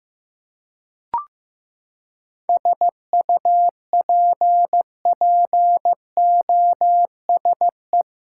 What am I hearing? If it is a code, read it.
Morse code, heard as a keyed tone, about 16 wpm, 700 Hz. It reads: SUPPOSE
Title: Seven Letter Words - 15wpm